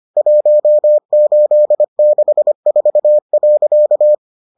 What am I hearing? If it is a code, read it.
Morse code, heard as a keyed tone, about 25 wpm, 600 Hz. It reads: 1864.